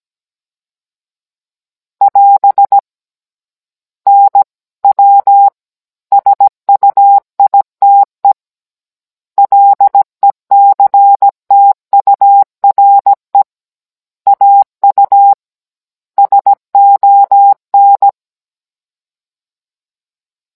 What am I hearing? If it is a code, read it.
Morse code, heard as a keyed tone, about 17 wpm, 800 Hz. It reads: <AS> NW SUITE LECTURE AU SON
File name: Cours-NR18-16mm